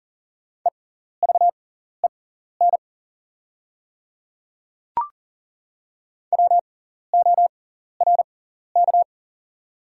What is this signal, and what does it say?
Morse code, heard as a keyed tone, about 40 wpm, spaced as for 14 wpm, 690 Hz. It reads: EVEN WORK
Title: 6x Character Spacing - Top 300 Words - 40wpm